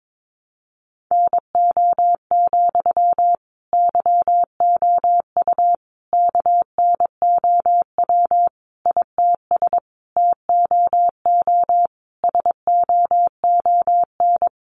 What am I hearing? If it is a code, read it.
Morse code, heard as a keyed tone, about 22 wpm, 700 Hz. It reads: NO, YOU KNOW ITS TOO SOON
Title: Sentences from Top 800 Words 22wpm